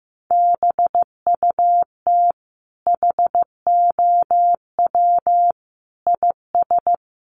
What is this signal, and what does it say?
Morse code, heard as a keyed tone, about 15 wpm, 700 Hz. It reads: BUT HOW IS
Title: Sentences from Top 100 Words 15wpm